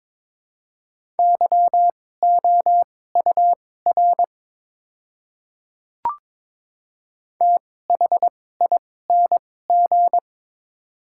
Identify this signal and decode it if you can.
Morse code, heard as a keyed tone, about 22 wpm, spaced as for 16 wpm, 700 Hz. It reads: YOUR THING
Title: 2x Character Spacing - Top 100 Words - 22wpm